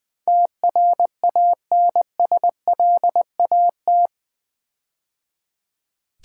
Morse code, heard as a keyed tone, about 20 wpm, 700 Hz. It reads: TRANSLAT
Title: Build A Word - 4 Letters At A Time - 11-Letter Words - 20wpm